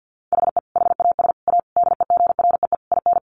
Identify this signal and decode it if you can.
Morse code, heard as a keyed tone, about 50 wpm, 700 Hz. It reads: HE HAS A DEGREE IN
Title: Sentences from Top 1500 Words 50wpm